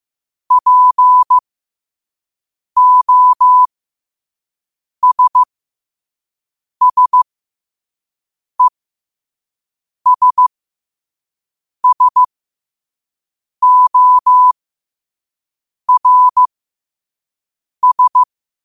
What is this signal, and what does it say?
Morse code, heard as a keyed tone, about 15 wpm, spaced as for 5 wpm, 1000 Hz. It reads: POSSESSORS
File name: qotdfiles / 5wpm